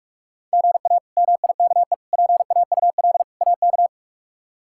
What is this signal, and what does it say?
Morse code, heard as a keyed tone, about 46 wpm, 700 Hz. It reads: GA MIKE PAUL AK
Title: K1USN SST QSO Practice - 45wpm